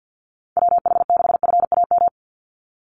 Morse code, heard as a keyed tone, about 48 wpm, 700 Hz. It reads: WH6FAM
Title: USA Call Signs 50wpm